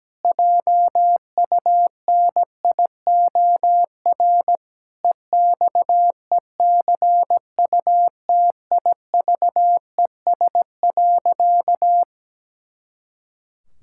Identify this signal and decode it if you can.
Morse code, heard as a keyed tone, about 17 wpm, 700 Hz. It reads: JUNIOR EXECUTIVES.
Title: Pangram Challenge v2 - 17wpm